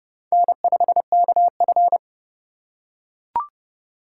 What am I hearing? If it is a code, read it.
Morse code, heard as a keyed tone, about 30 wpm, 700 Hz. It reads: N5KF
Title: USA Call Signs - 4 Characters - 2x Character Spacing - 30wpm